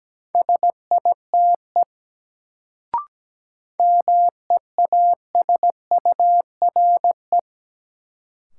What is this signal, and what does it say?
Morse code, heard as a keyed tone, about 17 wpm, 700 Hz. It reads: SITE MEASURE
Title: Sentences from Top 1000 Words - Review Part 1 - New Word Drill - 17wpm